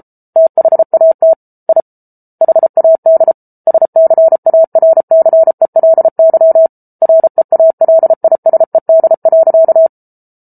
Morse code, heard as a keyed tone, about 34 wpm, 650 Hz. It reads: THAT I HAD SCARCELY REALISED.